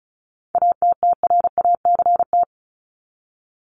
Morse code, heard as a keyed tone, about 34 wpm, 700 Hz. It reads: ATTRACT
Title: Top 1501-1750 Words 35wpm